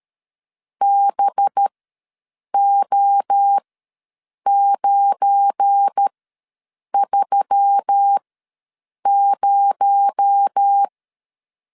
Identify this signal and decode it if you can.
Morse code, heard as a keyed tone, about 13 wpm, spaced as for 7 wpm, 800 Hz. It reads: BO930
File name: LEZIONI / 7 GF9Z / 7 GF9 ws 12wpm 3344